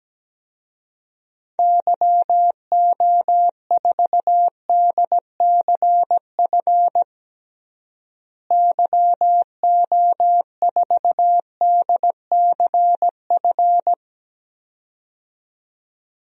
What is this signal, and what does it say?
Morse code, heard as a keyed tone, about 17 wpm, 700 Hz. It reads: YO4DCF YO4DCF
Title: DXCC-100 Call Signs - 6 Characters - Repeated 2x - 17wpm